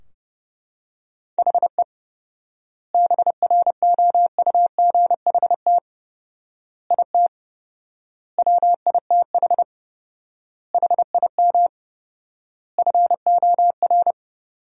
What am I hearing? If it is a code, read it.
Morse code, heard as a keyed tone, about 32 wpm, 700 Hz. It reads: HE BROUGHT IT WITH HIM FOR
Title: 4x Word-Spacing - Sentences from Top 500 Words - 30wpm